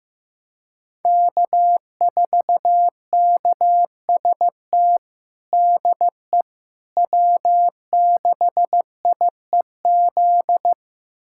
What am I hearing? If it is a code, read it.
Morse code, heard as a keyed tone, about 15 wpm, 700 Hz. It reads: K4KST DE W6IEZ
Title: USA Callsign DE Callsign - 5 Characters - 15wpm